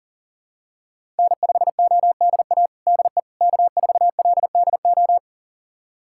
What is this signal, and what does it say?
Morse code, heard as a keyed tone, about 37 wpm, 700 Hz. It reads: N5ODA DE K4LDO
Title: USA Callsign DE Callsign - 5 Characters - 40wpm